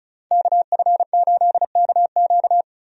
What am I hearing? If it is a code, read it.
Morse code, heard as a keyed tone, about 36 wpm, 700 Hz. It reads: KF8KQ
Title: USA Call Signs - 5 Characters - 35wpm